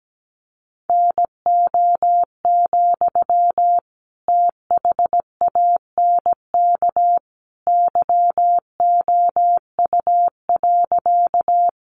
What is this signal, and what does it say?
Morse code, heard as a keyed tone, about 17 wpm, 700 Hz. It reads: NO, THANK YOU.
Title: Sentences from Top 600 Words 17wpm